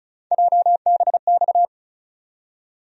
Morse code, heard as a keyed tone, about 36 wpm, 700 Hz. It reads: JBX
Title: USA Call Sign Postfixes - 3 Letters 35wpm